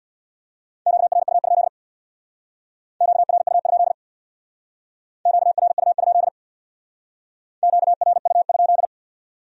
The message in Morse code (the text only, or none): QRU? QRU? QRU? QRU?